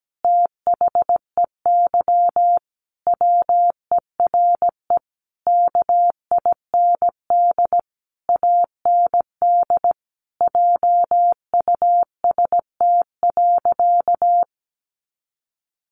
Morse code, heard as a keyed tone, about 17 wpm, 700 Hz. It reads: THEY WERE KIND AND JUST.
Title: Sentences from Top 800 Words 17wpm